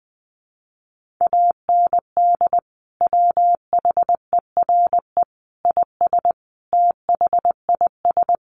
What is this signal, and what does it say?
Morse code, heard as a keyed tone, about 20 wpm, 700 Hz. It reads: AND WHERE IS THIS
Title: Sentences from Top 200 Words 20wpm